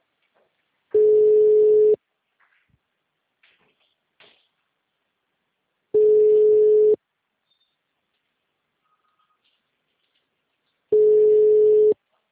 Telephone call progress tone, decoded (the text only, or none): ringback tone